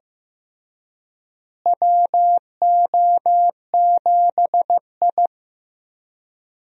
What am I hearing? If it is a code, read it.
Morse code, heard as a keyed tone, about 15 wpm, 700 Hz. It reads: WO7I
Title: USA Call Signs - 4 Characters - Repeated 3x - 15wpm